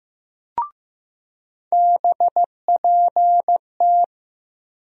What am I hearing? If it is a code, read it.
Morse code, heard as a keyed tone, about 16 wpm, 700 Hz. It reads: BPT
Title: USA Call Sign Postfixes - 3 Letters 15wpm